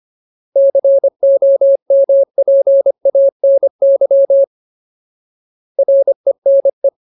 Morse code, heard as a keyed tone, about 25 wpm, 550 Hz. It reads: COMPANY RENE
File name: Ultimate Word List / 25WPM Ultimate